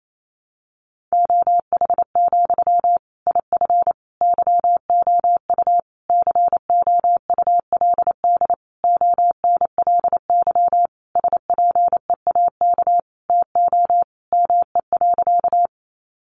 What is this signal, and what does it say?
Morse code, heard as a keyed tone, about 28 wpm, 700 Hz. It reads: OH, IF YOU COULD ONLY SPEAK TO ME.
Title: Sentences from Top 1100 Words 28wpm